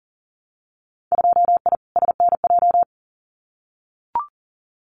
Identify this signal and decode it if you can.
Morse code, heard as a keyed tone, about 40 wpm, 700 Hz. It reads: 2I SNJ
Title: Winter Field Day Exchanges Encore 40wpm